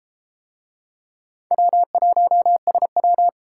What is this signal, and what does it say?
Morse code, heard as a keyed tone, about 32 wpm, 700 Hz. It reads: W1SW